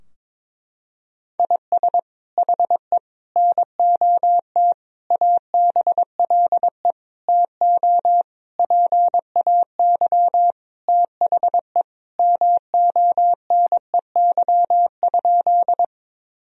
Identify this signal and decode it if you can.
Morse code, heard as a keyed tone, about 21 wpm, 700 Hz. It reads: IS HE NOT ABLE TO PAY THE MONEY?
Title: Sentences from Top 1000 Words - Review Part 2 - 22wpm